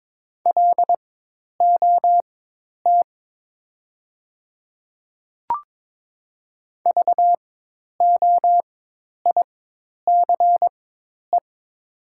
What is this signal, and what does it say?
Morse code, heard as a keyed tone, about 22 wpm, spaced as for 10 wpm, 700 Hz. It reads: LOT VOICE
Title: 4x Character Spacing - Top 500 Words - 22wpm